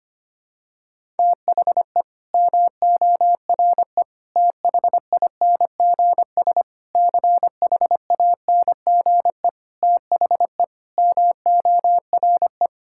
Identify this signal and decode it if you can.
Morse code, heard as a keyed tone, about 25 wpm, 700 Hz. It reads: THE MORE THINGS CHANGE THE MORE
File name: practice / 25wpm / Sentences with 9-Word Phrases - 25wpm